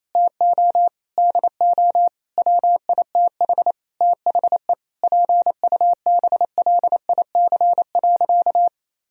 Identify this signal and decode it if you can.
Morse code, heard as a keyed tone, about 27 wpm, 700 Hz. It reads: TO DO WITH THE PUBLIC.
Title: Sentences from Top 1000 Words - Review Part 3 - 28wpm